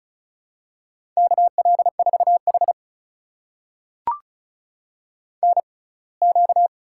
Morse code, heard as a keyed tone, about 36 wpm, spaced as for 24 wpm, 700 Hz. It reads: KL4H NQ
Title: USA Call Signs - 4 Characters - 6x Character Spacing - 35wpm